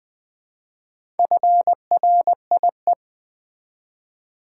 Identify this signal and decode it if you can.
Morse code, heard as a keyed tone, about 20 wpm, 700 Hz. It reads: FRIE
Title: Build A Word - 4 Letters At A Time - 10-Letter Words - 20wpm